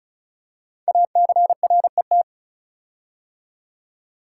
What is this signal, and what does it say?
Morse code, heard as a keyed tone, about 36 wpm, 690 Hz. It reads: ACRET
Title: Code Groups - 5 Letters - 35wpm